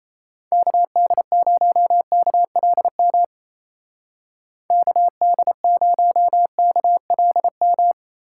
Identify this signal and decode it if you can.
Morse code, heard as a keyed tone, about 33 wpm, 700 Hz. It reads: KD0KLM KD0KLM
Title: Speed Racing - USA 2x3 Call Signs 28wpm